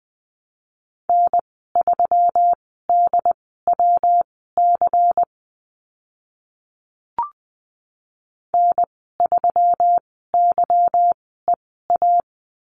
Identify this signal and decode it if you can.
Morse code, heard as a keyed tone, about 20 wpm, spaced as for 15 wpm, 700 Hz. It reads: N3DWC N3YEA
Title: 186 Lesson 30g - Letter b - Call Signs 20wpm